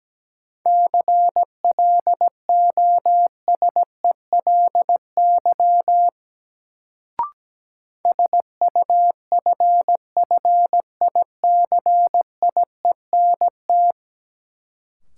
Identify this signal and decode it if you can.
Morse code, heard as a keyed tone, about 17 wpm, 700 Hz. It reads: CLOSELY SUFFICIENT
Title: Sentences from Top 1400 Words - New Word Drill - 17wpm